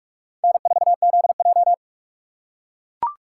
Morse code, heard as a keyed tone, about 44 wpm, 700 Hz. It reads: N4ZJ
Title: USA Call Signs - 4 Characters - 6x Character Spacing - 45wpm